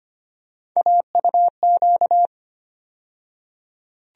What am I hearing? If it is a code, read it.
Morse code, heard as a keyed tone, about 25 wpm, 700 Hz. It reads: AUQ
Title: USA Call Sign Postfixes - 3 Letters 25wpm